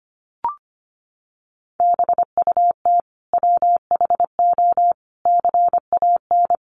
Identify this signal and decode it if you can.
Morse code, heard as a keyed tone, about 25 wpm, 700 Hz. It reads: BUT WHO CAN
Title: Sentences from Top 1500 Words 25wpm